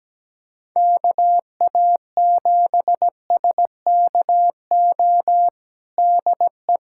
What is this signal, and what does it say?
Morse code, heard as a keyed tone, about 17 wpm, 700 Hz. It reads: KA7SKO DE
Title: USA Callsign DE Callsign - 6 Characters - 17wpm